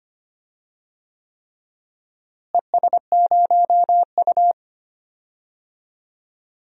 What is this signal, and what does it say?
Morse code, heard as a keyed tone, about 25 wpm, 700 Hz. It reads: ES0U
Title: DXCC-100 Call Signs - 4 Characters - Repeated 3x - 25wpm